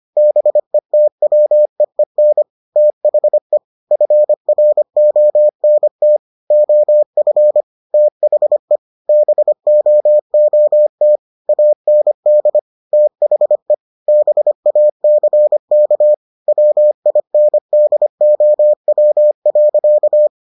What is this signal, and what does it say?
Morse code, heard as a keyed tone, about 25 wpm, 600 Hz. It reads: BETWEEN THE FRONT OF THE BOOT AND THE BACK WINDOW.